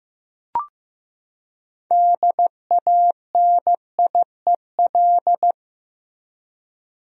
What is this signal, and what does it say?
Morse code, heard as a keyed tone, about 15 wpm, 700 Hz. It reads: DANIEL
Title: USA Common Names - 15wpm